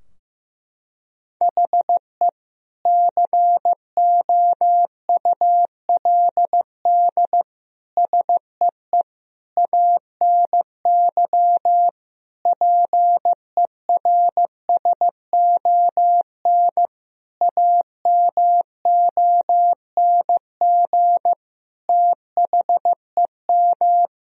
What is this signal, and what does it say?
Morse code, heard as a keyed tone, about 15 wpm, 700 Hz. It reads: HE COULD SEE ANY PERSON AMONG THEM
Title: Sentences from Top 1000 Words - Review Part 3 - 15wpm